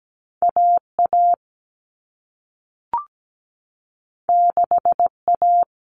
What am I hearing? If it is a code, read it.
Morse code, heard as a keyed tone, about 17 wpm, 700 Hz. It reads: AA 6A
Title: All Cut Numbers - 2-Digit 17wpm